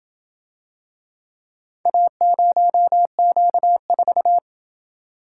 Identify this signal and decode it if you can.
Morse code, heard as a keyed tone, about 28 wpm, 700 Hz. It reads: A0Q4